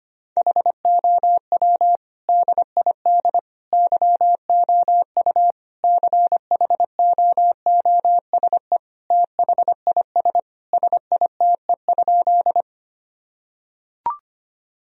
Morse code, heard as a keyed tone, about 25 wpm, 700 Hz. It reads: HOW DID YOU CHOOSE THIS SITE?
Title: Sentences from Top 1000 Words - Review Part 4 - 25wpm